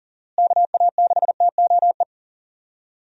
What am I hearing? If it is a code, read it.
Morse code, heard as a keyed tone, about 38 wpm, 700 Hz. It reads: KA6TOE
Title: USA Call Signs - 6 Characters - Repeated 2x - 40wpm